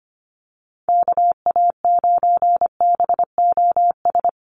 Morse code, heard as a keyed tone, about 25 wpm, 700 Hz. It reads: KA9BOS
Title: USA Call Signs Repeated 3x 25wpm